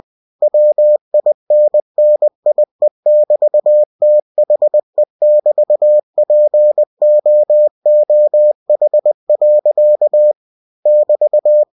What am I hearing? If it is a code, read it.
Morse code, heard as a keyed tone, about 20 wpm, 600 Hz. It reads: WINNIE<BT>THE<BT>POOH. <BT>
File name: cw-books / winnie-the-pooh / winnie-20wpm-page-0027